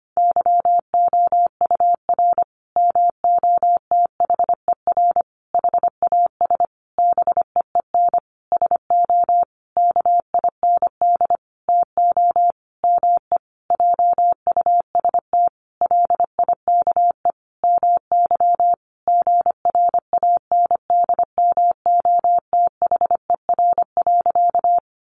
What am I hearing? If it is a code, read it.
Morse code, heard as a keyed tone, about 25 wpm, 700 Hz. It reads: YOUR MOTHER HAS BEEN SO KIND TO ME JUST LIKE MY GRANDMOTHER.
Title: Sentences from Top 1200 Words 25wpm